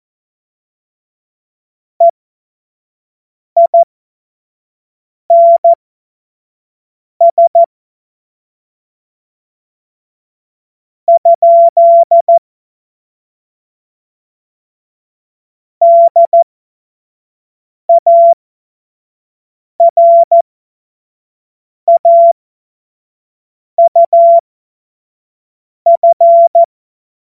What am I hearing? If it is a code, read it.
Morse code, heard as a keyed tone, about 14 wpm, spaced as for 5 wpm, 680 Hz. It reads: EINS ? DARAUF
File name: Morsekurs / Witze / Witz 04. Mai 2021, 25 F70